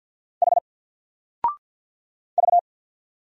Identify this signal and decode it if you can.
Morse code, heard as a keyed tone, about 48 wpm, 700 Hz. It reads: H V
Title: Single Letter-Number 50wpm